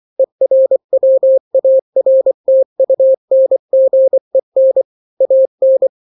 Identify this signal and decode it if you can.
Morse code, heard as a keyed tone, about 24 wpm, 530 Hz. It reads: ERWARTUNGEN AN
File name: daily / 23wpm